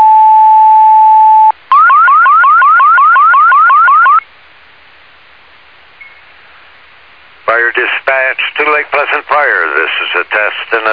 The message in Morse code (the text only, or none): none